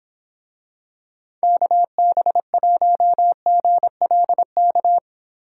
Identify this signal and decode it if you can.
Morse code, heard as a keyed tone, about 26 wpm, 700 Hz. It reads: KB1GLK